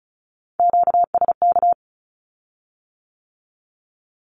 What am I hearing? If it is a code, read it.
Morse code, heard as a keyed tone, about 36 wpm, 700 Hz. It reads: QSK
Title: Q-Codes with Spoken Meaning - 35wpm